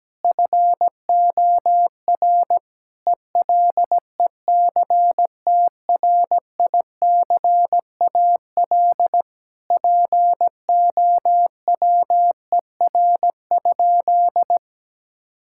Morse code, heard as a keyed tone, about 17 wpm, 700 Hz. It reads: FOR ELECTRICAL POWER?
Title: Sentences from Top 1000 Words - Review Part 6 - 17wpm